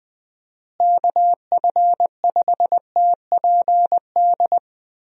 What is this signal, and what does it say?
Morse code, heard as a keyed tone, about 20 wpm, 700 Hz. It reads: KF5TPD
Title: USA Call Signs - 6 Characters - Repeated 2x - 20wpm